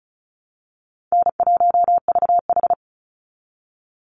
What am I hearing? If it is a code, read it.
Morse code, heard as a keyed tone, about 34 wpm, 700 Hz. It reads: N1VH
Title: Easy Call Signs Repeated 3x - 35wpm